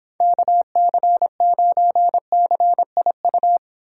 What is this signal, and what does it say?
Morse code, heard as a keyed tone, about 26 wpm, 700 Hz. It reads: KC9CIU